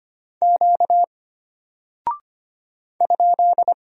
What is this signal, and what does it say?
Morse code, heard as a keyed tone, about 25 wpm, 700 Hz. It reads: Q ?
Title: Single Character (Letter:Number:Punctuation:Prosign) Alt 25wpm